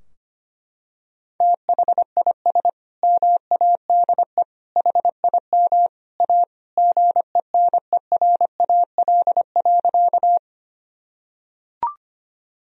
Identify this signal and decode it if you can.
Morse code, heard as a keyed tone, about 25 wpm, 700 Hz. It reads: THIS MADE HIM A GENERAL.
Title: Sentences from Top 1000 Words - Review Part 3 - 25wpm